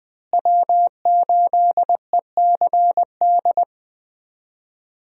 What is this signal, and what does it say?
Morse code, heard as a keyed tone, about 20 wpm, 700 Hz. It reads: W8ECD
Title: USA Call Signs - 5 Characters - Repeated 3x - 20wpm